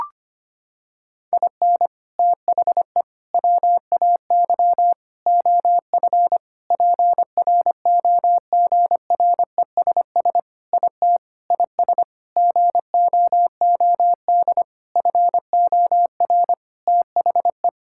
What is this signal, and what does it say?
Morse code, heard as a keyed tone, about 25 wpm, 700 Hz. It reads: IN THE WAY OF PROGRESS IT IS GOOD FOR THE
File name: practice / 25wpm / Sentences with 4-Word Phrases - 25wpm